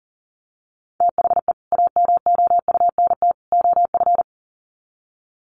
Morse code, heard as a keyed tone, about 40 wpm, 700 Hz. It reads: THE AMOUNT OF
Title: Sets of 3 Words 40wpm